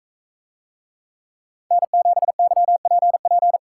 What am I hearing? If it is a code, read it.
Morse code, heard as a keyed tone, about 42 wpm, 700 Hz. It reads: N7YPP